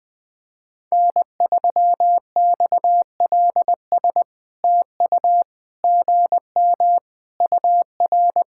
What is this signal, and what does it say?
Morse code, heard as a keyed tone, about 20 wpm, 700 Hz. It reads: N3XLS TU GM UR
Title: POTA QSO Practice - 20wpm